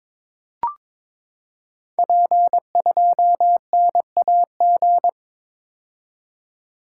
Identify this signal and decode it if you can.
Morse code, heard as a keyed tone, about 22 wpm, 700 Hz. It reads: P2NAG
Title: Code Groups - 5 Letter:Numbers - 22wpm